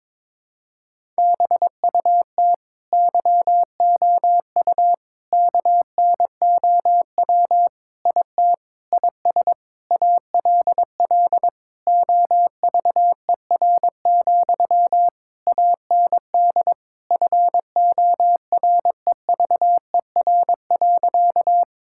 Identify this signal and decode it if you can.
Morse code, heard as a keyed tone, about 22 wpm, 700 Hz. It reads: BUT YOU KNOW IT IS ALL OVER, AND FOREVER.
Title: Sentences from Top 1600 Words 22wpm